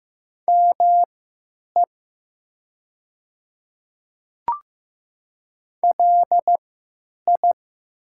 Morse code, heard as a keyed tone, about 15 wpm, spaced as for 9 wpm, 700 Hz. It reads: ME LI
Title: 3x Character Spacing - Top 200 Words - 15wpm